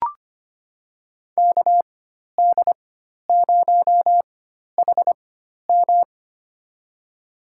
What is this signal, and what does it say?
Morse code, heard as a keyed tone, about 25 wpm, spaced as for 12 wpm, 700 Hz. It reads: KD0HM
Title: USA Call Signs - 5 Characters - 4x Character Spacing - 25wpm